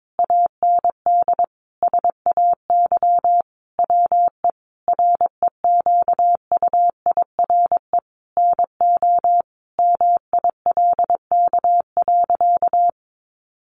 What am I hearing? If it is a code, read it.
Morse code, heard as a keyed tone, about 22 wpm, 700 Hz. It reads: AND SAY WE REQUIRE NO MILK.
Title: Sentences from Top 1500 Words 22wpm